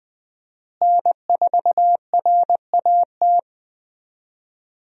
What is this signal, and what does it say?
Morse code, heard as a keyed tone, about 20 wpm, 700 Hz. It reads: N4RAT
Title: USA Call Signs - 5 Characters - Repeated 2x - 20wpm